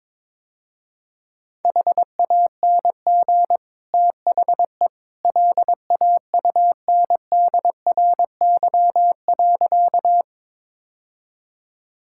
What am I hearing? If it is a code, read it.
Morse code, heard as a keyed tone, about 22 wpm, 700 Hz. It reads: HANG THE LAUNDRY.
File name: practice / 22wpm / Easy Sentences with Hints - 22wpm